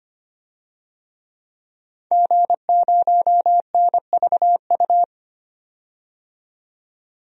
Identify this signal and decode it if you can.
Morse code, heard as a keyed tone, about 25 wpm, 700 Hz. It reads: G0NVU
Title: DXCC-100 Call Signs - 5 Characters - 25wpm